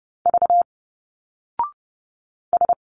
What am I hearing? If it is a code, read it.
Morse code, heard as a keyed tone, about 30 wpm, 700 Hz. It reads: V S
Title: Single Letters 30wpm